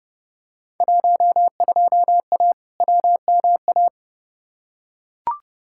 Morse code, heard as a keyed tone, about 32 wpm, 700 Hz. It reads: 12A WMA
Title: Field Day Exchanges Encore - 30wpm